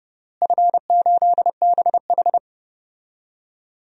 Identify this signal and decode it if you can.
Morse code, heard as a keyed tone, about 30 wpm, 700 Hz. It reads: F8BH
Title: DXCC-100 Call Signs - 4 Characters - Repeated 3x - 30wpm